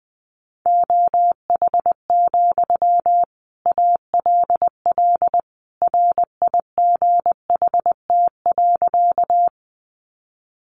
Morse code, heard as a keyed tone, about 20 wpm, 700 Hz. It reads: OH, ALL RIGHT.
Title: Sentences from Top 500 Words 20wpm